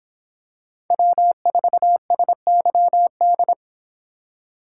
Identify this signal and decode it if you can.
Morse code, heard as a keyed tone, about 26 wpm, 700 Hz. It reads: W4SYD